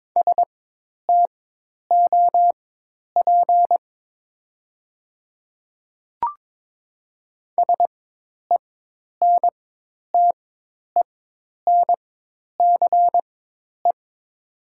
Morse code, heard as a keyed tone, about 22 wpm, spaced as for 10 wpm, 700 Hz. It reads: STOP SENTENCE
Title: 4x Character Spacing - Top 300 Words - 22wpm